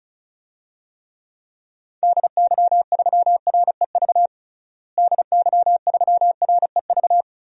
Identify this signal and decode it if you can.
Morse code, heard as a keyed tone, about 34 wpm, 700 Hz. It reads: DY3REV DY3REV
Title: DXCC-100 Call Signs - 6 Characters - Repeated 3x - 35wpm